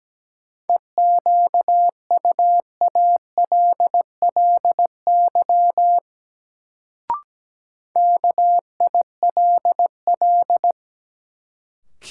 Morse code, heard as a keyed tone, about 17 wpm, 700 Hz. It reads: EQUALLY KILL
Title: Sentences from Top 1400 Words - New Word Drill - 17wpm